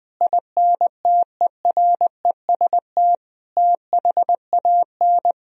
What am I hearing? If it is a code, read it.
Morse code, heard as a keyed tone, about 20 wpm, 700 Hz. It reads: INTEREST THAN
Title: Sentences from Top 400 Words and Abbreviations 20wpm